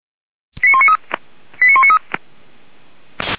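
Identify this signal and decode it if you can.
Five-tone selective call (ZVEI), heard as quick runs of five tones, tones of about 70 ms. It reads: 89193 89193
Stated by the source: Five-tone ZVEI selective call with code 89193, received by SDR.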